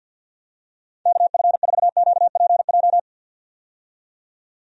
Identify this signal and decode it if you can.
Morse code, heard as a keyed tone, about 52 wpm, 700 Hz. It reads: KF4QPJ
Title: USA Call Signs - 6 Characters - 50wpm